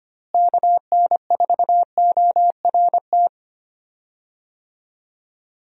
Morse code, heard as a keyed tone, about 25 wpm, 700 Hz. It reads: KN4ORT
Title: USA Call Signs - 6 Characters - Repeated 2x - 25wpm